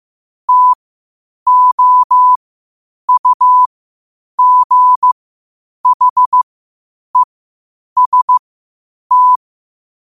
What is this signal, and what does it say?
Morse code, heard as a keyed tone, about 15 wpm, spaced as for 8 wpm, 1000 Hz. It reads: TOUGHEST
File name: qotdfiles / 8wpm